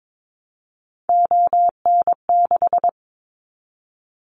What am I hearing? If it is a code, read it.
Morse code, heard as a keyed tone, about 22 wpm, 700 Hz. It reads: ON6
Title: Callsign Prefixes for Top 25 DXCC - 22wpm